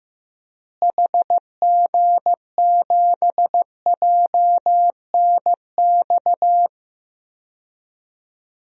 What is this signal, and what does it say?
Morse code, heard as a keyed tone, about 15 wpm, 700 Hz. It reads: HG7JNX
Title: DXCC-100 Call Signs - 6 Characters - Repeated 3x - 15wpm